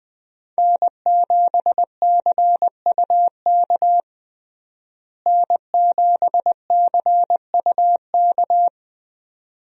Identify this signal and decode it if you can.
Morse code, heard as a keyed tone, about 20 wpm, 700 Hz. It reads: N7CUK N7CUK
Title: USA Call Signs - 5 Characters - Repeated 2x - 20wpm